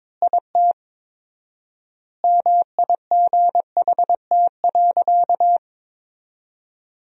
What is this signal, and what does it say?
Morse code, heard as a keyed tone, about 22 wpm, 700 Hz. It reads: IT MIGHT.
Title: 4x Word-Spacing - Sentences from Top 500 Words - 22wpm